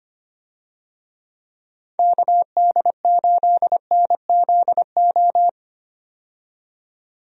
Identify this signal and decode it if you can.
Morse code, heard as a keyed tone, about 25 wpm, 700 Hz. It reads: KD8NZO
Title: USA Call Signs - 6 Characters - 25wpm